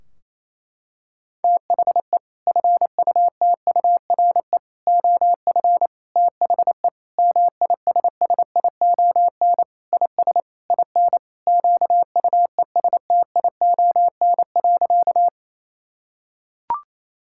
Morse code, heard as a keyed tone, about 28 wpm, 700 Hz. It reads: THE FUTURE OF THE MISSION IS IN QUESTION.
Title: Sentences from Top 1600 Words 28wpm